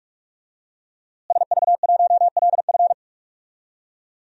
Morse code, heard as a keyed tone, about 46 wpm, 700 Hz. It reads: SV1LF
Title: DXCC-100 Call Signs - 5 Characters - Repeated 3x - 45wpm